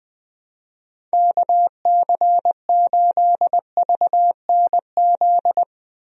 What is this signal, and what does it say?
Morse code, heard as a keyed tone, about 20 wpm, 700 Hz. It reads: KC8VNZ
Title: USA Call Signs - 6 Characters - 20wpm